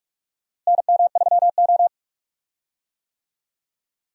Morse code, heard as a keyed tone, about 46 wpm, 700 Hz. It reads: NM3O
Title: USA Call Signs - 4 Characters - Repeated 3x - 45wpm